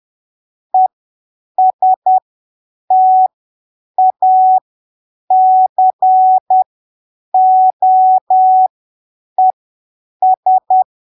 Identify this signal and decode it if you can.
Morse code, heard as a keyed tone, about 10 wpm, spaced as for 7 wpm, 750 Hz. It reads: ESTACOES
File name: arquivos / texto / texto 03 10ppm